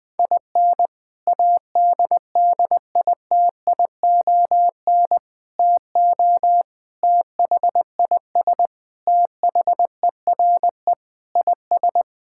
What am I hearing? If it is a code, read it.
Morse code, heard as a keyed tone, about 21 wpm, 700 Hz. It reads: IN ADDITION TO THIS THERE IS
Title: Sentences with 3-Word Phrases - Spoken Beginning - 20wpm